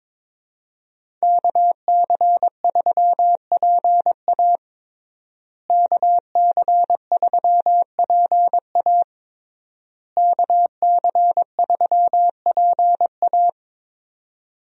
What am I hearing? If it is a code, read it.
Morse code, heard as a keyed tone, about 22 wpm, 700 Hz. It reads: KC3PA KC3PA KC3PA
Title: USA Call Signs - 5 Characters - Repeated 3x - 22wpm